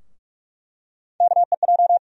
Morse code, heard as a keyed tone, about 46 wpm, 700 Hz. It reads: KEJ
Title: USA Call Sign Postfixes - 3 Letters 45wpm